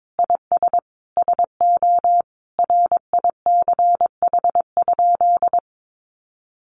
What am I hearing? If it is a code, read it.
Morse code, heard as a keyed tone, about 22 wpm, 700 Hz. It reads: IS SO RICH?
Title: Sentences from Top 1100 Words 22wpm